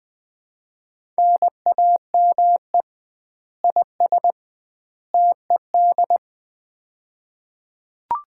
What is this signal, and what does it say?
Morse code, heard as a keyed tone, about 20 wpm, 700 Hz. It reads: NAME IS TED
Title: Common QSO Phrases - 2x Word-Spacing - 20wpm